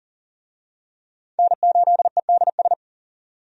Morse code, heard as a keyed tone, about 40 wpm, 700 Hz. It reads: N8EDS
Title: USA Call Signs - 5 Characters - 40wpm